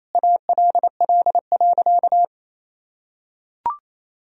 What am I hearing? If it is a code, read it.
Morse code, heard as a keyed tone, about 28 wpm, 700 Hz. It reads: ALL.
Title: Sentences from Top 1000 Words - Review Part 1 - 28wpm